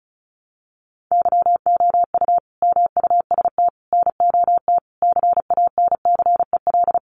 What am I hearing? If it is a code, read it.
Morse code, heard as a keyed tone, about 34 wpm, 700 Hz. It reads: YOU MUST NOT CANCEL
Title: Sentences from Top 1200 Words 35wpm